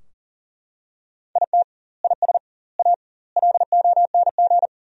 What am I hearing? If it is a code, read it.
Morse code, heard as a keyed tone, about 37 wpm, 700 Hz. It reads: IT IS A LONG
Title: 2x Word-Spacing - Sentences from Top 300 Words - 40wpm